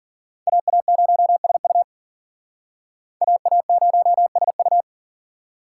Morse code, heard as a keyed tone, about 51 wpm, 700 Hz. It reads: AA0SU AA0SU